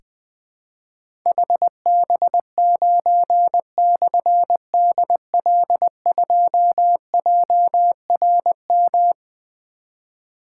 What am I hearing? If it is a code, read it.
Morse code, heard as a keyed tone, about 20 wpm, 700 Hz. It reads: HB9/DL2JRM
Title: Difficult Call Signs 20wpm